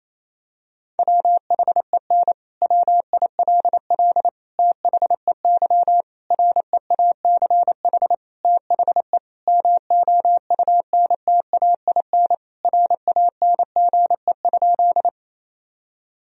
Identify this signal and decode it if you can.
Morse code, heard as a keyed tone, about 28 wpm, 700 Hz. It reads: WHEN WILL THEY REACH THE MOUNTAIN RANGE?
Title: Sentences from Top 1000 Words - Review Part 4 - 28wpm